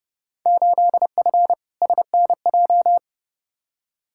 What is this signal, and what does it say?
Morse code, heard as a keyed tone, about 30 wpm, 700 Hz. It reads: 8F SNJ
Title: Field Day Exchanges Encore - 30wpm